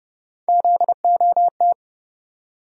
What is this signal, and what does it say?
Morse code, heard as a keyed tone, about 31 wpm, 700 Hz. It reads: ZOT
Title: USA Call Sign Postfixes - 3 Letters 30wpm